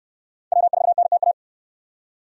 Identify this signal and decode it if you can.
Morse code, heard as a keyed tone, about 69 wpm, 700 Hz. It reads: W4NTA